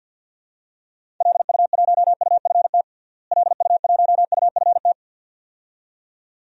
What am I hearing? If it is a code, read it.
Morse code, heard as a keyed tone, about 52 wpm, 700 Hz. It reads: LU1UFT LU1UFT
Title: DXCC-100 Call Signs - 6 Characters - Repeated 2x - 50wpm